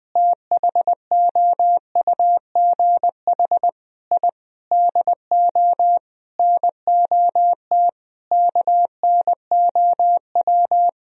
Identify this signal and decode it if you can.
Morse code, heard as a keyed tone, about 20 wpm, 700 Hz. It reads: THOUGH I DO NOT KNOW
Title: Sentences from Top 1000 Words - Review Part 2 - 20wpm